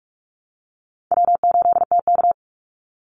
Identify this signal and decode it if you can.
Morse code, heard as a keyed tone, about 44 wpm, 700 Hz. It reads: W8TK
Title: Easy Call Signs Repeated 2x - 45wpm